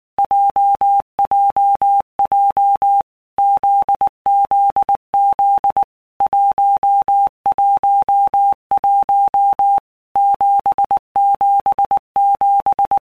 Morse code, heard as a keyed tone, about 19 wpm, 800 Hz. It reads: JJJ ZZZ 111 777
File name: All Groups 1-minute Blocks / Groups@18 / JZ17